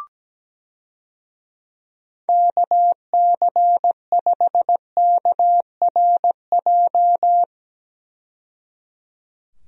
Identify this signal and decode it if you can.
Morse code, heard as a keyed tone, about 17 wpm, 700 Hz. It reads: KC5KRJ
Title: USA Call Signs - 6 Characters - 17wpm